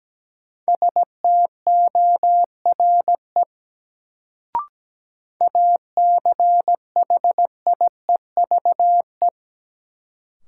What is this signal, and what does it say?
Morse code, heard as a keyed tone, about 17 wpm, 700 Hz. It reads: STORE ACHIEVE
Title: Sentences from Top 1000 Words - Review Part 5 - New Word Drill - 17wpm